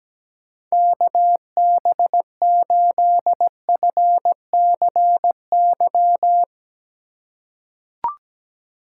Morse code, heard as a keyed tone, about 17 wpm, 700 Hz. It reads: KB8FCY
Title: USA Call Signs - 6 Characters - 3x Character Spacing - 17wpm